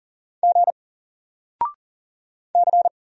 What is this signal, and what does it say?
Morse code, heard as a keyed tone, about 40 wpm, 700 Hz. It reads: G C
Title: Single Character (Letter:Number:Punctuation:Prosign) 40wpm